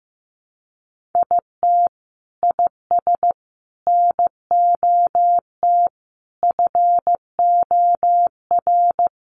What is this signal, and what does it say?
Morse code, heard as a keyed tone, about 15 wpm, 700 Hz. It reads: IT IS NOT FOR
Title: Sentences from Top 1500 Words 15wpm